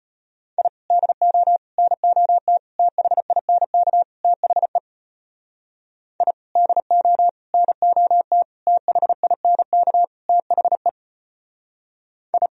Morse code, heard as a keyed tone, about 32 wpm, 700 Hz. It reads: I DO NOT THINK THE I DO NOT THINK THE I